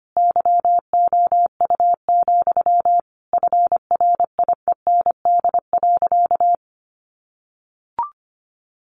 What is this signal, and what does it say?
Morse code, heard as a keyed tone, about 25 wpm, 700 Hz. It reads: YOU, FRIEND.
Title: Sentences from Top 1500 Words 25wpm